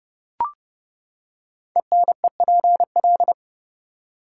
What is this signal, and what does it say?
Morse code, heard as a keyed tone, about 30 wpm, 700 Hz. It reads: ENEPL
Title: Code Groups - 5 Letters - 30wpm